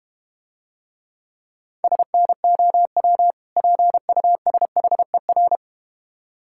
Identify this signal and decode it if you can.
Morse code, heard as a keyed tone, about 31 wpm, 700 Hz. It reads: SNOW PUSHER